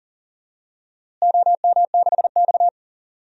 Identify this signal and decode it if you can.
Morse code, heard as a keyed tone, about 40 wpm, 700 Hz. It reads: OM6X
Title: DXCC-100 Call Signs - 4 Characters - Repeated 2x - 40wpm